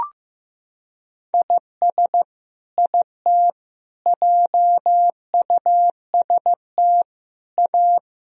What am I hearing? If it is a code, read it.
Morse code, heard as a keyed tone, about 16 wpm, 700 Hz. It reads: IS IT JUST A
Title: Sentences from Top 1000 Words - Review Part 2 - 15wpm